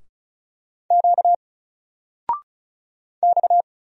Morse code, heard as a keyed tone, about 36 wpm, 700 Hz. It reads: Q X
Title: Single Letter-Number 35wpm